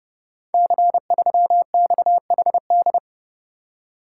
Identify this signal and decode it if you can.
Morse code, heard as a keyed tone, about 30 wpm, 700 Hz. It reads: C3XHD
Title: Code Groups - 5 Letter:Numbers - 30wpm